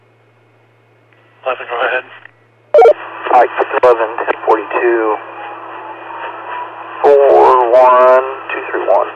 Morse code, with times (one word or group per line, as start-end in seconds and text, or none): none